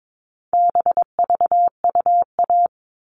Morse code, heard as a keyed tone, about 22 wpm, 700 Hz. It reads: BVUA
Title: All Cut Numbers - 4-Digit 22wpm